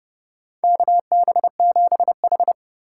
Morse code, heard as a keyed tone, about 32 wpm, 700 Hz. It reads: KB7H
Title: K1USN SST QSO Practice - 30wpm